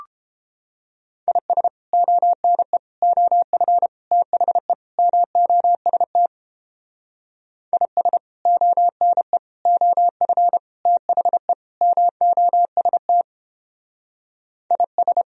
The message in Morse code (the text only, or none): IS ONE OF THE MOST IS ONE OF THE MOST IS